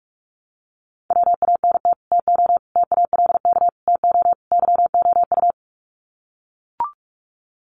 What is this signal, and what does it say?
Morse code, heard as a keyed tone, about 44 wpm, 700 Hz. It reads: WANT TO TALK TO YOU
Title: Sets of 5 Words Encore 45wpm